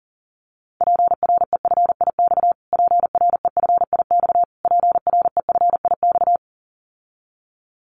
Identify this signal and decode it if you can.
Morse code, heard as a keyed tone, about 40 wpm, 700 Hz. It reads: PREFIX PREFIX PREFIX
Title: Callsign Prefixes for Top 25 DXCC - Repeated 3x - 40wpm